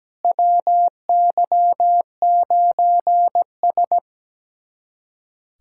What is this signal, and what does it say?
Morse code, heard as a keyed tone, about 17 wpm, 700 Hz. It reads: WY9S
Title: USA Callsign DE Callsign - 4 Characters - 17wpm